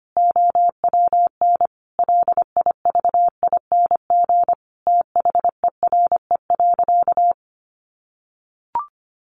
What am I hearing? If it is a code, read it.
Morse code, heard as a keyed tone, about 25 wpm, 700 Hz. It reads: OWN LIVING THERE.
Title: Sentences from Top 1500 Words 25wpm